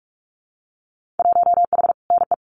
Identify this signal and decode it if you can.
Morse code, heard as a keyed tone, about 46 wpm, 700 Hz. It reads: 1H NE
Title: Winter Field Day Exchanges Encore 45wpm